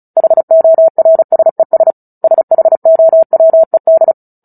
Morse code, heard as a keyed tone, about 34 wpm, 650 Hz. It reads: HORSES SHOWED
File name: Book Two / 09-Chapter 9 - 35 WPM